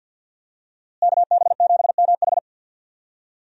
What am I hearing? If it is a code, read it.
Morse code, heard as a keyed tone, about 51 wpm, 700 Hz. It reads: KB7MH
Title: USA Call Signs - 5 Characters - Repeated 3x - 50wpm